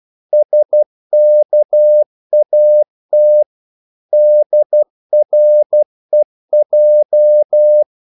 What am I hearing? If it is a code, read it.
Morse code, heard as a keyed tone, about 12 wpm, 600 Hz. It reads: SKAT DREJ
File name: Korte ord DK 12wpm